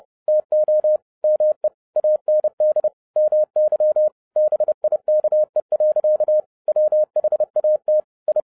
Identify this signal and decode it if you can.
Morse code, heard as a keyed tone, about 30 wpm, 610 Hz. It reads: TO ME AND MY BIKE. WHAT I
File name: 2011-07-mp3 / n55 BACK FROM A BIG ACCIDENT 30wpm